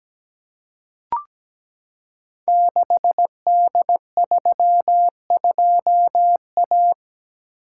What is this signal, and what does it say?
Morse code, heard as a keyed tone, about 17 wpm, 700 Hz. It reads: 6D32A
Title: Code Groups - 5 Letter:Numbers - 17wpm